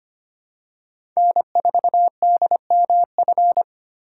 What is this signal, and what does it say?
Morse code, heard as a keyed tone, about 25 wpm, 700 Hz. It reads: N4DMF
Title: USA Call Signs - 3x Character Spacing - 25wpm